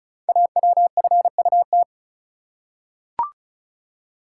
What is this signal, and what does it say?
Morse code, heard as a keyed tone, about 36 wpm, 700 Hz. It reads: AWFUT
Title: Code Groups - 5 Letters - 35wpm